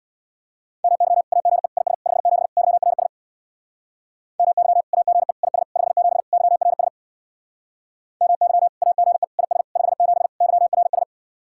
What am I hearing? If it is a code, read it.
Morse code, heard as a keyed tone, about 60 wpm, 700 Hz. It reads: MY AGE IS 57 YRS MY AGE IS 57 YRS MY AGE IS 57 YRS
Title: Speed-Racing Common QSO Phrases Encore - 50wpm